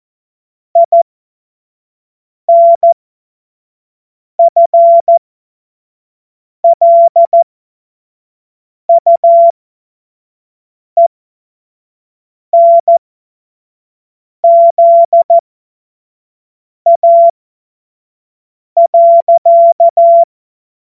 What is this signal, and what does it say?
Morse code, heard as a keyed tone, about 14 wpm, spaced as for 5 wpm, 680 Hz. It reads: INFLUENZA.
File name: Morsekurs / Witze / Witz 06. Juli 2021, 25 F70